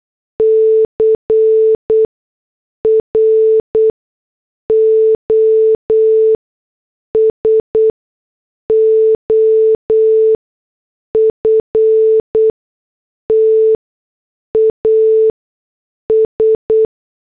Morse code, heard as a keyed tone, about 8 wpm, spaced as for 6 wpm, 440 Hz. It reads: CROSOFT AS